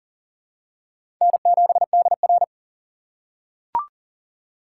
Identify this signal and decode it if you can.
Morse code, heard as a keyed tone, about 40 wpm, 700 Hz. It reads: N7DR
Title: K1USN SST QSO Practice - 40wpm